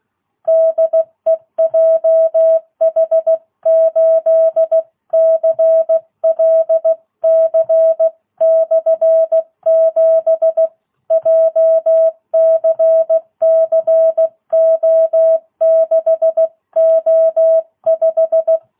Morse code, heard as a keyed tone, about 16 wpm, 660 Hz. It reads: DE JH8CLC/7 JCCO6O5
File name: jh8cbh / 2022cw